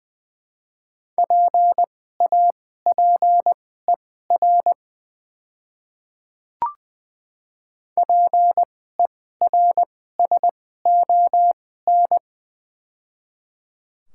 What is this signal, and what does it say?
Morse code, heard as a keyed tone, about 20 wpm, spaced as for 14 wpm, 700 Hz. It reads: PAPER PERSON
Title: 2x Character Spacing - Top 500 Words - 20wpm